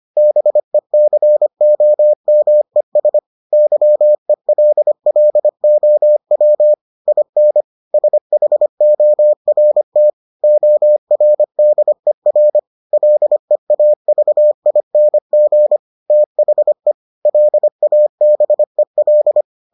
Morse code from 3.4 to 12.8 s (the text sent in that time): YELLOW IN SHORT ORDER